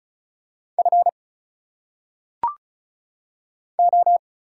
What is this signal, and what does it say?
Morse code, heard as a keyed tone, about 34 wpm, 700 Hz. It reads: F O
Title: Single Letters - Phonetic Pronunciation 35wpm